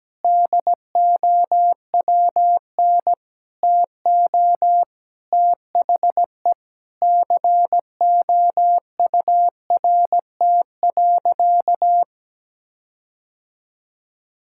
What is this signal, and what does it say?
Morse code, heard as a keyed tone, about 17 wpm, 700 Hz. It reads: DOWN TO THE COURT.
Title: Sentences from Top 1600 Words 17wpm